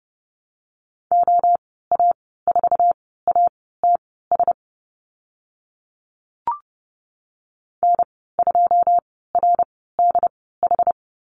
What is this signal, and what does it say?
Morse code, heard as a keyed tone, about 30 wpm, spaced as for 17 wpm, 700 Hz. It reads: OA4ATS N2RDH
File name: Lesson 14g - Number 2 - Call Signs 30wpm